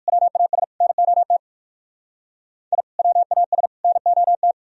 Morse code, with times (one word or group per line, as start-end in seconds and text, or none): 0.1-0.6 WAS
0.8-1.4 NOT
2.7-2.8 I
3.0-3.7 WAS
3.8-4.5 NOT